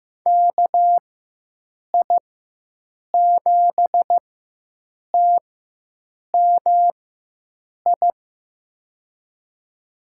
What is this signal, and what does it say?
Morse code, heard as a keyed tone, about 15 wpm, spaced as for 7 wpm, 700 Hz. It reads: KI7TMI
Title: USA Call Signs - 4x Character Spacing - 15wpm